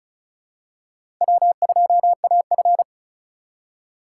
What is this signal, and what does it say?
Morse code, heard as a keyed tone, about 36 wpm, 700 Hz. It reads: W2AF
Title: USA Call Signs - 4 Characters - 4x Character Spacing - 35wpm